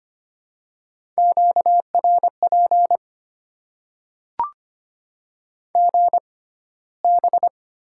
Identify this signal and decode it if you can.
Morse code, heard as a keyed tone, about 25 wpm, spaced as for 15 wpm, 700 Hz. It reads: QRP GB
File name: Top 90 QSO Elements - 6x Character Spacing - 25wpm